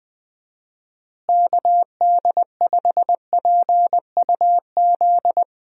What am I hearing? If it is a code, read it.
Morse code, heard as a keyed tone, about 20 wpm, 700 Hz. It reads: KD5PUZ
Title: USA Call Signs - 6 Characters - Repeated 2x - 20wpm